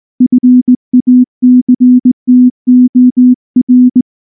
Morse code, heard as a keyed tone, about 18 wpm, 250 Hz. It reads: FACTOR
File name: morse / morse-training-20WPM-1